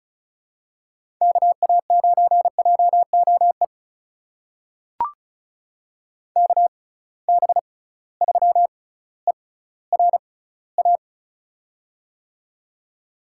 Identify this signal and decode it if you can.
Morse code, heard as a keyed tone, about 36 wpm, spaced as for 18 wpm, 700 Hz. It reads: KA9JOE KB3ERA
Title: USA Call Signs - 6x Character Spacing - 35wpm